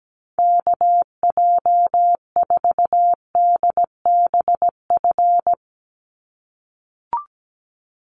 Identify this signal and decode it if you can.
Morse code, heard as a keyed tone, about 17 wpm, 700 Hz. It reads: KJ4DBF
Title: USA Call Signs 17wpm